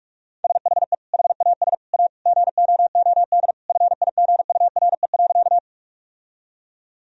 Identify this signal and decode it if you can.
Morse code, heard as a keyed tone, about 46 wpm, 700 Hz. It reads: SHE HAS A GOOD FIGURE.
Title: Sentences from Top 1000 Words - Review Part 2 - 45wpm